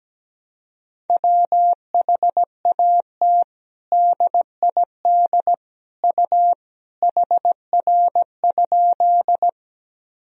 Morse code, heard as a keyed tone, about 17 wpm, 700 Hz. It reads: WHAT DID U HR?
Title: Sentences from Top 300 Words and Abbreviations 17wpm